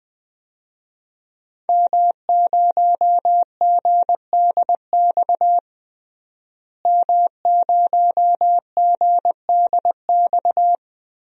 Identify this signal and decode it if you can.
Morse code, heard as a keyed tone, about 20 wpm, 700 Hz. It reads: M0GDX M0GDX
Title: DXCC-100 Call Signs - 5 Characters - Repeated 2x - 20wpm